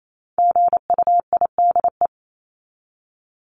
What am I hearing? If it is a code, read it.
Morse code, heard as a keyed tone, about 28 wpm, 700 Hz. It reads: GUIDE
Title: Top 501-750 Words 28wpm